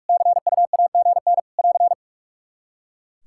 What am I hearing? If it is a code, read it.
Morse code, heard as a keyed tone, about 46 wpm, 700 Hz. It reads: CUAGN <AR>
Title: Practice QSO - Exchange 4 - Ending - 45wpm